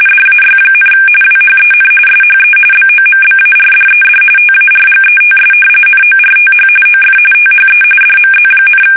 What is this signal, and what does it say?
RTTY, about 45.45 baud. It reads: DAY, WHEN SHE BOMBARDED TARGETS NORTH OF THE DMZ,
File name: fsk850 / battleshipnj